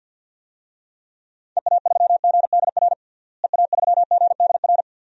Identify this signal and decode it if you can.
Morse code, heard as a keyed tone, about 47 wpm, 700 Hz. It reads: EA3GDR EA3GDR
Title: DXCC-100 Call Signs - 6 Characters - Repeated 2x - 50wpm